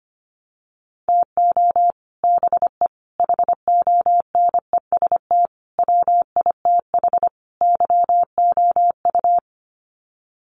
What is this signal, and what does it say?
Morse code, heard as a keyed tone, about 25 wpm, 700 Hz. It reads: TO BE HONEST WITH YOU
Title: Sets of 5 Words Encore 25wpm